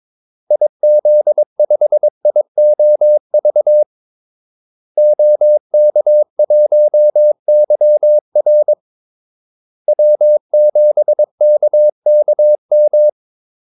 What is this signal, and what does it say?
Morse code, heard as a keyed tone, about 22 wpm, 600 Hz. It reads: IZ5IOV OK1YR W7KKM
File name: Callsigns-22wpm